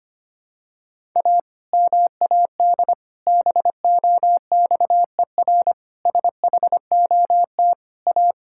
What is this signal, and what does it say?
Morse code, heard as a keyed tone, about 25 wpm, 700 Hz. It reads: A MAD BOXER SHOT A
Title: Pangram Challenge v2 - 25wpm